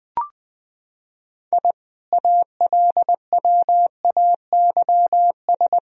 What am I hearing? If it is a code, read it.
Morse code, heard as a keyed tone, about 20 wpm, 700 Hz. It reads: I ALWAYS
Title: Sentences from Top 1000 Words - Review Part 3 - 20wpm